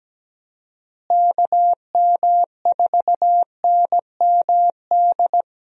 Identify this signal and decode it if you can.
Morse code, heard as a keyed tone, about 17 wpm, 700 Hz. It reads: KM4NMD
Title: USA Call Signs - 6 Characters - Repeated 2x - 17wpm